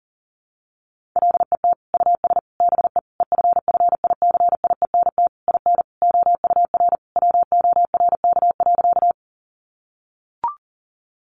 Morse code, heard as a keyed tone, about 40 wpm, 700 Hz. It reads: LET US BE EFFICIENT IN OUR WORK.
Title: Sentences from Top 1300 Words 40wpm